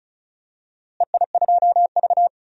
Morse code, heard as a keyed tone, about 36 wpm, 700 Hz. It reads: EI2V
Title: DXCC-100 Call Signs - 4 Characters - Repeated 2x - 35wpm